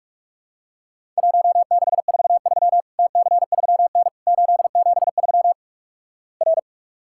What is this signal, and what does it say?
Morse code, heard as a keyed tone, about 46 wpm, 700 Hz. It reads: 1643 TC3N 873 R
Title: Worked All Europe DX Contest - QTC Exchange - 45wpm